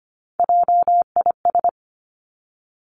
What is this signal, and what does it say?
Morse code, heard as a keyed tone, about 25 wpm, 700 Hz. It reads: JIS
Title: Three Challenging Letter-Number Combinations 25wpm